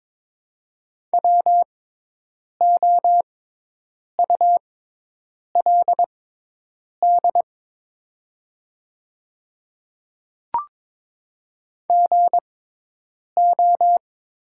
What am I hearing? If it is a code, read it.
Morse code, heard as a keyed tone, about 22 wpm, spaced as for 8 wpm, 700 Hz. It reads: WOULD GO
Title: 6x Character Spacing - Top 100 Words - 22wpm